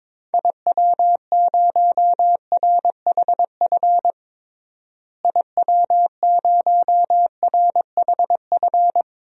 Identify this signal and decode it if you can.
Morse code, heard as a keyed tone, about 22 wpm, 700 Hz. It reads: IW0RHF IW0RHF
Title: DXCC-100 Call Signs - 6 Characters - Repeated 3x - 22wpm